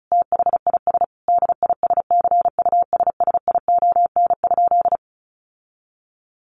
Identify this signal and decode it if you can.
Morse code, heard as a keyed tone, about 34 wpm, 700 Hz. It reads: THIS DISCUSSION?
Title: Sentences from Top 1100 Words 35wpm